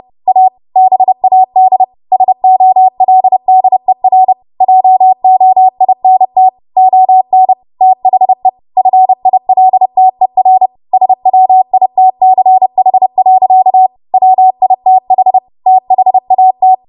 Morse code, heard as a keyed tone, about 30 wpm, 750 Hz. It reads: A BAD SOLDER JOINT ON THE FILTER SWITCH. WITH THAT
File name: Archive / 30 WPM / 30WPM